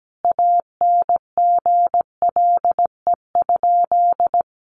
Morse code, heard as a keyed tone, about 17 wpm, 700 Hz. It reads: ANGLE?
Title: Sentences from Top 1500 Words 17wpm